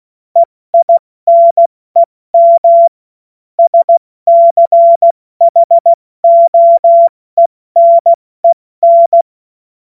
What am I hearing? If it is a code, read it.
Morse code, heard as a keyed tone, about 16 wpm, 680 Hz. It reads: EINEM SCHOENEN
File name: Morsekurs / Witze / Witz 14.November 2023 70 F80